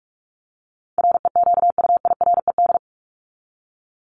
Morse code, heard as a keyed tone, about 44 wpm, 700 Hz. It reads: REQUIRED
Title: ED Suffix - Top 100 Words with Base Words 45wpm